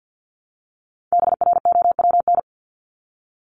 Morse code, heard as a keyed tone, about 48 wpm, 700 Hz. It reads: BROWN
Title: Top 1001-1250 Words 50wpm